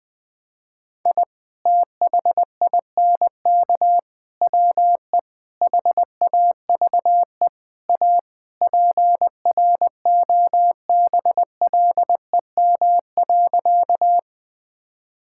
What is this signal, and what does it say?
Morse code, heard as a keyed tone, about 20 wpm, 700 Hz. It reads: I THINK WE HAVE A PROBLEM.
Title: Sentences with 4-Word Phrases - 20wpm